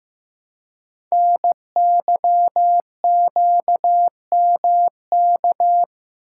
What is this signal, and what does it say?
Morse code, heard as a keyed tone, about 15 wpm, 700 Hz. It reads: NYQMK
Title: Code Groups - 5 Letters - 15wpm